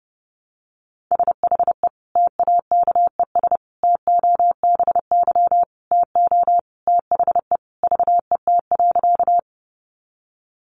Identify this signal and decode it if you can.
Morse code, heard as a keyed tone, about 30 wpm, 700 Hz. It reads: SHE TAKES TOBY TO THE VET.
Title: Easy Phrases 30wpm